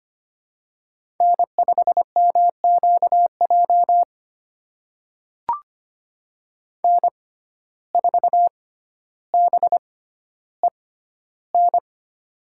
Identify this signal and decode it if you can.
Morse code, heard as a keyed tone, about 25 wpm, spaced as for 13 wpm, 700 Hz. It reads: N5MQJ N4BEN
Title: USA Call Signs - 5 Characters - 6x Character Spacing - 25wpm